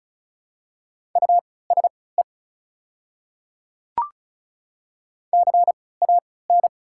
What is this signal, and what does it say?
Morse code, heard as a keyed tone, about 36 wpm, spaced as for 20 wpm, 690 Hz. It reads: USE CAN
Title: 3x Character Spacing - Top 100 Words - 35wpm